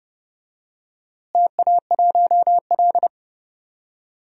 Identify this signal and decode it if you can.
Morse code, heard as a keyed tone, about 32 wpm, 700 Hz. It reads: TA1L
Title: DXCC-100 Call Signs - 4 Characters - Repeated 2x - 30wpm